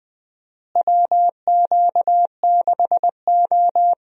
Morse code, heard as a keyed tone, about 20 wpm, 700 Hz. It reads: WQ6O
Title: USA Call Signs - 4 Characters - Repeated 3x - 20wpm